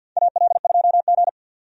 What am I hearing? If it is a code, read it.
Morse code, heard as a keyed tone, about 52 wpm, 700 Hz. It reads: AL2G
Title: USA Call Signs - 4 Characters - 50wpm